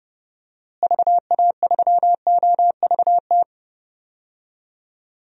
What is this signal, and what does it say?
Morse code, heard as a keyed tone, about 29 wpm, 700 Hz. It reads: VA3OVT
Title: DXCC-100 Call Signs - 6 Characters - Repeated 3x - 30wpm